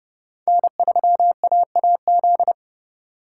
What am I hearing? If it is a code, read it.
Morse code, heard as a keyed tone, about 31 wpm, 700 Hz. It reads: N3AAZ